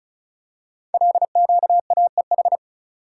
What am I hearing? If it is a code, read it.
Morse code, heard as a keyed tone, about 36 wpm, 700 Hz. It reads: LQAEH
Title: Code Groups - 5 Letters - 35wpm